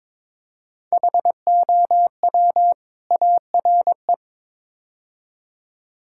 Morse code, heard as a keyed tone, about 22 wpm, 700 Hz. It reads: HOW ARE
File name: practice / 22wpm / Build A Phrase - 5-Word Phrases - 22wpm